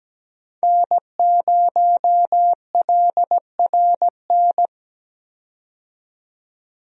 Morse code, heard as a keyed tone, about 17 wpm, 700 Hz. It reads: N0LRN
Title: USA Call Signs - 5 Characters - 17wpm